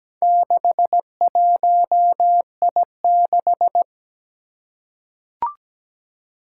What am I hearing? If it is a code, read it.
Morse code, heard as a keyed tone, about 17 wpm, 700 Hz. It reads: 61I6
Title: Four Challenging Letter-Number Combinations 17wpm